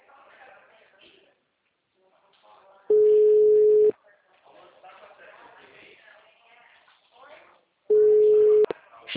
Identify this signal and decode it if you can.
Ringback tone (Europe, 425 Hz), with speech.
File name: out- -2007- - - .93791